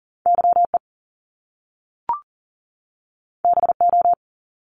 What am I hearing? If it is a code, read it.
Morse code, heard as a keyed tone, about 42 wpm, 700 Hz. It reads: YE BO
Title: Top Two Letter Combinations in Words 40wpm